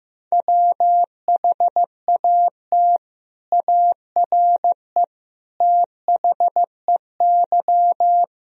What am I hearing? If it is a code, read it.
Morse code, heard as a keyed tone, about 15 wpm, 700 Hz. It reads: WHAT ARE THEY
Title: Sentences from Top 1000 Words - Review Part 3 - 15wpm